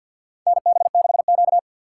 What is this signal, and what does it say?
Morse code, heard as a keyed tone, about 52 wpm, 700 Hz. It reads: NB6Q
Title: USA Call Signs - 4 Characters - 6x Character Spacing - 50wpm